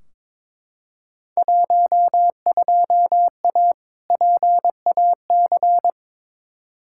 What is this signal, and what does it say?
Morse code, heard as a keyed tone, about 22 wpm, 700 Hz. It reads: 12A PAC
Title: Field Day Exchanges Encore - 22wpm